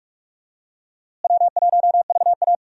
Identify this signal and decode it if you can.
Morse code, heard as a keyed tone, about 47 wpm, 700 Hz. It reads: W1VA
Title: USA Call Signs - 4 Characters - Repeated 3x - 45wpm